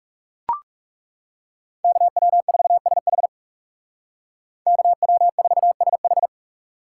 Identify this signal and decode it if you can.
Morse code, heard as a keyed tone, about 40 wpm, 700 Hz. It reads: KW4SH KW4SH